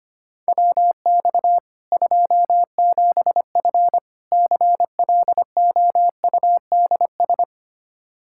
Morse code, heard as a keyed tone, about 25 wpm, 700 Hz. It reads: WX 27F CLOUDS
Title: K1USN SST QSO Practice 25wpm